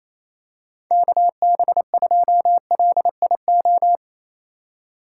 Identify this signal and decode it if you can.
Morse code, heard as a keyed tone, about 28 wpm, 700 Hz. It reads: KB2LIO
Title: USA Call Signs - 3x Character Spacing - 28wpm